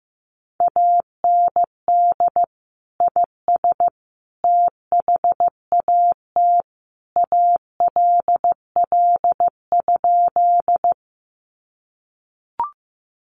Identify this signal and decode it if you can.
Morse code, heard as a keyed tone, about 15 wpm, 700 Hz. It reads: AND IS THAT ALL?
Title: Sentences from Top 100 Words 15wpm